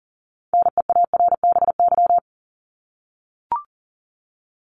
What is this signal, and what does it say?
Morse code, heard as a keyed tone, about 40 wpm, 700 Hz. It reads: NEARBY
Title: Top 1501-1750 Words 40wpm